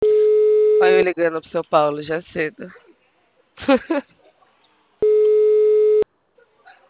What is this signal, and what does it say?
Ringback tone (Europe, 425 Hz), with speech.
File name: out- -2004- - - .4465